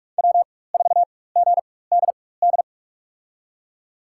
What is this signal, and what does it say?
Morse code, heard as a keyed tone, about 46 wpm, spaced as for 21 wpm, 700 Hz. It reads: W4GDD
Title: USA Call Signs - 5 Characters - 4x Character Spacing - 45wpm